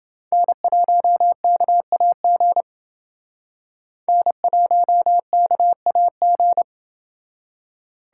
Morse code, heard as a keyed tone, about 29 wpm, 700 Hz. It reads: N1KAG N1KAG